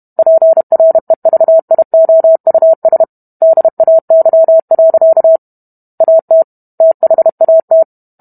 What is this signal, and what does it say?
Morse code, heard as a keyed tone, about 32 wpm, 650 Hz. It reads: PREVIOUS DAY. AT THAT